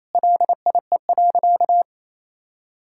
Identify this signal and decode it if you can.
Morse code, heard as a keyed tone, about 28 wpm, 700 Hz. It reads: LIE.
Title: Sentences from Top 1600 Words 28wpm